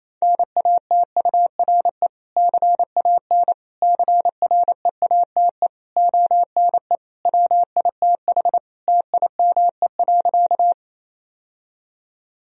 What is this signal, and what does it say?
Morse code, heard as a keyed tone, about 27 wpm, 700 Hz. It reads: NATURE CAN CREATE ONE WITH TIME.
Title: Sentences from Top 1000 Words - Review Part 3 - 28wpm